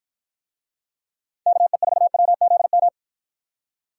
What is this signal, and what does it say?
Morse code, heard as a keyed tone, about 48 wpm, 700 Hz. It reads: KE4WZM